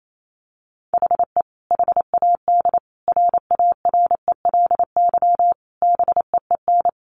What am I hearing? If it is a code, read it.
Morse code, heard as a keyed tone, about 28 wpm, 700 Hz. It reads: HE HAD RARELY BEEN
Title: Sentences from Top 1400 Words 28wpm